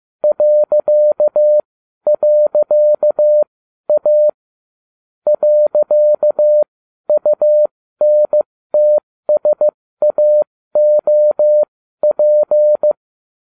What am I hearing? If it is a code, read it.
Morse code, heard as a keyed tone, about 15 wpm, spaced as for 12 wpm, 600 Hz. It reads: ..A .UNTSAOP